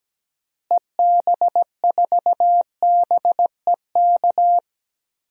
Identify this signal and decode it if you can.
Morse code, heard as a keyed tone, about 17 wpm, 700 Hz. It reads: EB4BEK
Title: DXCC-100 Call Signs - 6 Characters - Repeated 3x - 17wpm